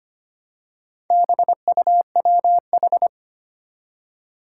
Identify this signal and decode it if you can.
Morse code, heard as a keyed tone, about 25 wpm, 700 Hz. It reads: BUWH
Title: Four Challenging Letter-Number Combinations 25wpm